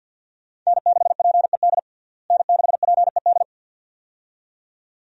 Morse code, heard as a keyed tone, about 49 wpm, 700 Hz. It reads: N6PED N6PED
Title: USA Call Signs - 5 Characters - Repeated 2x - 50wpm